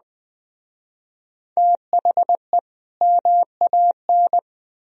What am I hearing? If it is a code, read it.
Morse code, heard as a keyed tone, about 20 wpm, 700 Hz. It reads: THE MAN
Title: Speed-Racing - Sets of 2 Words 17wpm